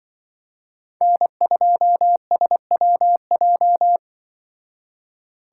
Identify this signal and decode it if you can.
Morse code, heard as a keyed tone, about 25 wpm, 700 Hz. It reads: N2SWJ
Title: Speed Racing - USA 1x3 Call Signs 20wpm